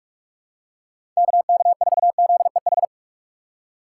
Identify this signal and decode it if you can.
Morse code, heard as a keyed tone, about 46 wpm, 700 Hz. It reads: KK4ZEH
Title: USA Call Signs - 6 Characters - 4x Character Spacing - 45wpm